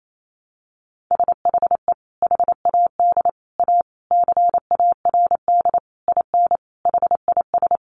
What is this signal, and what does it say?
Morse code, heard as a keyed tone, about 28 wpm, 700 Hz. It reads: SHE HAD A CARD IN HIS
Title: Sentences from Top 600 Words 28wpm